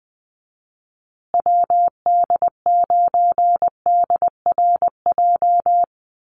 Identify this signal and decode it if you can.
Morse code, heard as a keyed tone, about 20 wpm, 700 Hz. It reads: WD9DRJ
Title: USA Call Signs Repeated 3x 20wpm